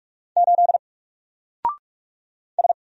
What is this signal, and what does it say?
Morse code, heard as a keyed tone, about 45 wpm, 700 Hz. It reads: 8 S
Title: Single Letter-Number Alt 45wpm